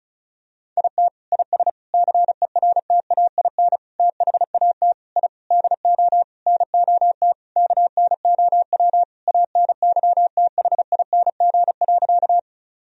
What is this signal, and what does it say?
Morse code, heard as a keyed tone, about 36 wpm, 700 Hz. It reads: IT IS CERTAIN THAT I DO NOT KNOW ANYTHING.
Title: Sentences from Top 1000 Words - Review Part 2 - 35wpm